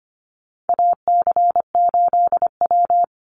Morse code, heard as a keyed tone, about 25 wpm, 700 Hz. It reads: AC8W
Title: Call Signs Repeated 2x - 25wpm